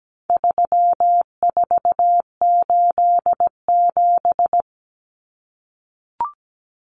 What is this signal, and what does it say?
Morse code, heard as a keyed tone, about 17 wpm, 700 Hz. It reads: 3487
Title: Numbers - 4-Digit 17wpm